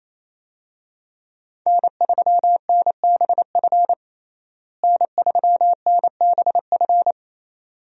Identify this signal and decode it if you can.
Morse code, heard as a keyed tone, about 28 wpm, 700 Hz. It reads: N3NBF N3NBF
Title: USA Call Signs - 5 Characters - Repeated 2x - 28wpm